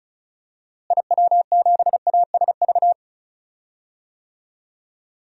Morse code, heard as a keyed tone, about 36 wpm, 700 Hz. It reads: IW7ASV
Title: DXCC-100 Call Signs - 6 Characters - Repeated 2x - 35wpm